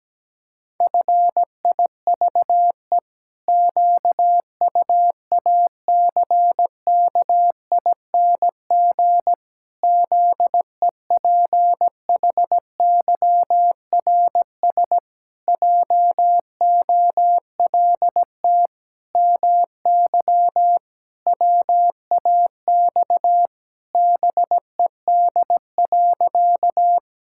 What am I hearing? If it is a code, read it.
Morse code, heard as a keyed tone, about 17 wpm, 700 Hz. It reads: FIVE QUACKING ZEPHYRS JOLT MY WAX BED.
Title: Pangram Challenge v2 - 17wpm